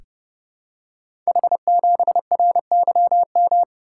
Morse code, heard as a keyed tone, about 29 wpm, 700 Hz. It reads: H7RYM
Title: Code Groups - 5 Letter:Numbers - 30wpm